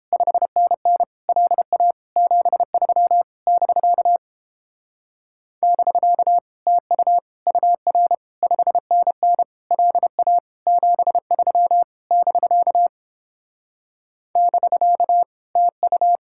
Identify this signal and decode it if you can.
Morse code, heard as a keyed tone, about 30 wpm, 700 Hz. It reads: 5NN LA 73 <BK> <BK> TU UR 5NN LA 73 <BK> <BK> TU